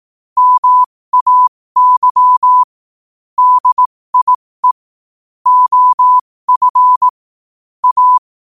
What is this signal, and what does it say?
Morse code, heard as a keyed tone, about 18 wpm, spaced as for 15 wpm, 1000 Hz. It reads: MAY DIE OF A
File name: qotdfiles / 15wpm